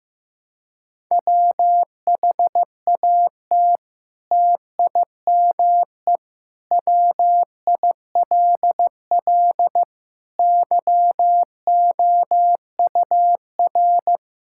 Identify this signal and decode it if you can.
Morse code, heard as a keyed tone, about 15 wpm, 700 Hz. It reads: WHAT TIME WILL YOUR
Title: Sentences from Top 1000 Words - Review Part 5 - 15wpm